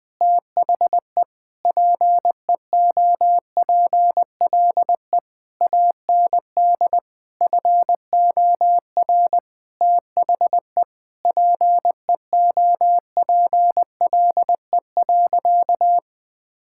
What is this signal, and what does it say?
Morse code, heard as a keyed tone, about 20 wpm, 700 Hz. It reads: THE PEOPLE AND FOR THE PEOPLE.
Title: Sentences with 8-Word Phrases - 20wpm